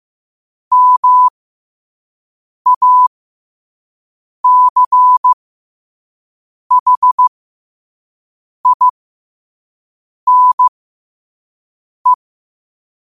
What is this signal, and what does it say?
Morse code, heard as a keyed tone, about 15 wpm, spaced as for 5 wpm, 1000 Hz. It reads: MACHINE
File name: qotdfiles / 5wpm